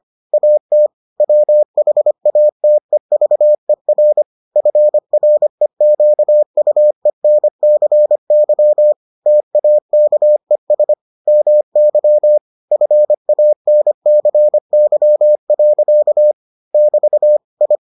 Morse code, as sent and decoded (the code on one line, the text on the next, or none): .- - / .-- .... .- - . ...- . .-. / ..-. .-. --.- ..- . -. -.-. -.-- / - .- -.- . ... / -- -.-- / ..-. .- -. -.-. -.-- .-.-.- / -...- / ..
AT WHATEVER FREQUENCY TAKES MY FANCY. <BT> I